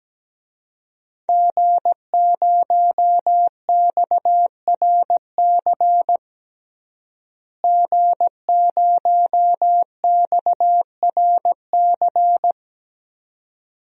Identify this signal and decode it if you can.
Morse code, heard as a keyed tone, about 17 wpm, 700 Hz. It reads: G0XRC G0XRC
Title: DXCC-100 Call Signs - 5 Characters - Repeated 2x - 17wpm